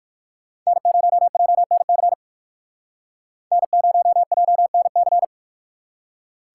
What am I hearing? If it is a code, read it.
Morse code, heard as a keyed tone, about 58 wpm, 700 Hz. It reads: N0JNC N0JNC